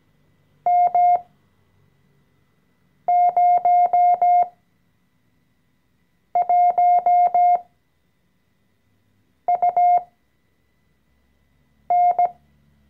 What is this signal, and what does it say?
Morse code, heard as a keyed tone, about 17 wpm, spaced as for 4 wpm, 710 Hz. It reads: M01UN
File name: CW-Online-Kurs / Anfänger / Lektion 014